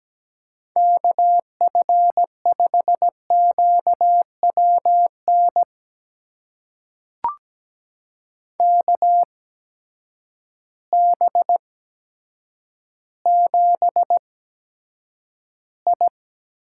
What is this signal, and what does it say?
Morse code, heard as a keyed tone, about 17 wpm, spaced as for 8 wpm, 700 Hz. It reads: KF5QWN KB7I
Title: USA Call Signs - 6 Characters - 8x Character Spacing - 17wpm